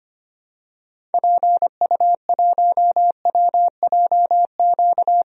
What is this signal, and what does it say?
Morse code, heard as a keyed tone, about 25 wpm, 700 Hz. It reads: PU1WJQ
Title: DXCC-100 Call Signs - 6 Characters - 25wpm